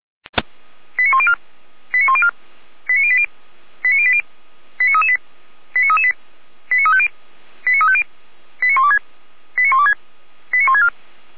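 Five-tone selective call (ZVEI), heard as quick runs of five tones, tones of about 70 ms. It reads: 89104 89104 89080 89080 89338 89338 89350 89350 89126 89126 89164
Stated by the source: Five-tone ZVEI selective call received by SDR.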